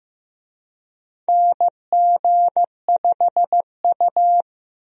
Morse code, heard as a keyed tone, about 15 wpm, 700 Hz. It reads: NG5U
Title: USA Call Signs - 4 Characters - 15wpm